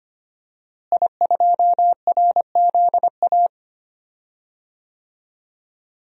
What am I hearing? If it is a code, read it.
Morse code, heard as a keyed tone, about 25 wpm, 700 Hz. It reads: I2RZA
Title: DXCC-100 Call Signs - 5 Characters - Repeated 2x - 25wpm